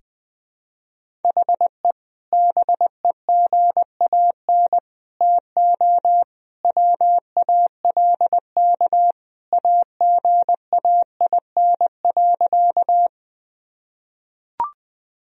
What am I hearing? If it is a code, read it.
Morse code, heard as a keyed tone, about 20 wpm, 700 Hz. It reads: HE BEGAN TO WALK AGAIN.
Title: Sentences from Top 1000 Words - Review Part 1 - 20wpm